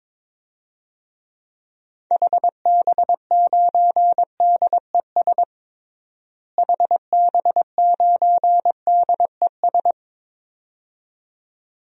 Morse code, heard as a keyed tone, about 22 wpm, 700 Hz. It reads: HB9DES HB9DES
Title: DXCC-100 Call Signs - 6 Characters - Repeated 2x - 22wpm